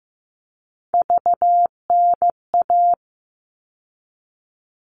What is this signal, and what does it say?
Morse code, heard as a keyed tone, about 15 wpm, 700 Hz. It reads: VNA
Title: All Cut Numbers - 3-Digit 15wpm